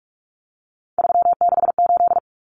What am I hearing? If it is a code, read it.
Morse code, heard as a keyed tone, about 44 wpm, 700 Hz. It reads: 368
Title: Numbers - 3-Digit 45wpm